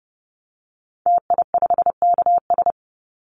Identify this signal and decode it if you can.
Morse code, heard as a keyed tone, about 30 wpm, 700 Hz. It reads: TI5KS
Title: Call Signs Repeated 3x - 30wpm